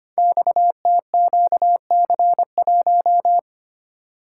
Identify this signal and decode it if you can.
Morse code, heard as a keyed tone, about 25 wpm, 700 Hz. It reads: XTQC1
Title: Code Groups - 5 Letter:Numbers - 25wpm